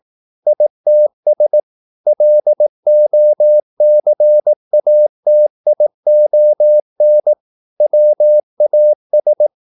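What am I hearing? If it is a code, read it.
Morse code, heard as a keyed tone, about 18 wpm, 600 Hz. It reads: ITS LOCATION WAS
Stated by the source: Morse practice at 18 wpm, English text, from CW med Gnister.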